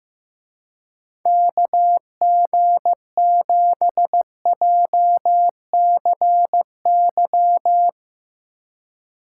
Morse code, heard as a keyed tone, about 15 wpm, 700 Hz. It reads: KG7JCY
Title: Speed Racing - USA Call Signs 15wpm